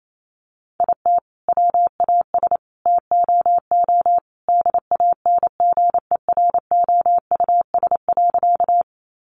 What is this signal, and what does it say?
Morse code, heard as a keyed tone, about 28 wpm, 700 Hz. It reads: IT WAS TOO DANGEROUS.
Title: Sentences from Top 1200 Words 28wpm